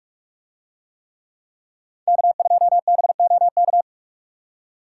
Morse code, heard as a keyed tone, about 45 wpm, 700 Hz. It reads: K2BOK